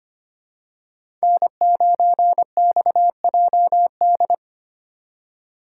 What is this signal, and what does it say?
Morse code, heard as a keyed tone, about 29 wpm, 700 Hz. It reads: N9XJD
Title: Speed Racing - USA 1x3 Call Signs 25wpm